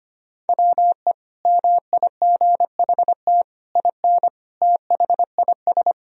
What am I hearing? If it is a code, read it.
Morse code, heard as a keyed tone, about 25 wpm, 700 Hz. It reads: WE MIGHT IN THIS
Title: Sentences from Top 1000 Words - Review Part 1 - 25wpm